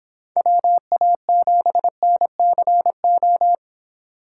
Morse code, heard as a keyed tone, about 26 wpm, 700 Hz. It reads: WA7NCO